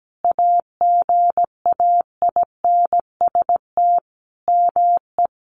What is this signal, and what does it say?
Morse code, heard as a keyed tone, about 17 wpm, 700 Hz. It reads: AGAINST ME
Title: Sentences from Top 400 Words 17wpm